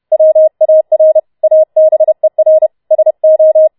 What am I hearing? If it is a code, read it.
Morse code, heard as a keyed tone, about 31 wpm, 630 Hz. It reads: WAR ABER SO